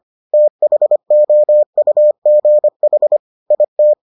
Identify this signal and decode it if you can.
Morse code, heard as a keyed tone, about 25 wpm, 600 Hz. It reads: THOUGH IT